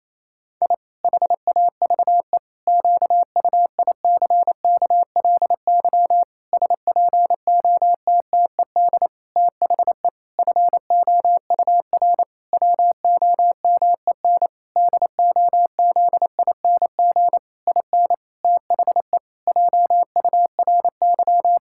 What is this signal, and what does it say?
Morse code, heard as a keyed tone, about 27 wpm, 700 Hz. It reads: I HAVE QUICKLY SPOTTED THE FOUR WOMEN DOZING IN THE JURY
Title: Pangram Challenge v2 - 28wpm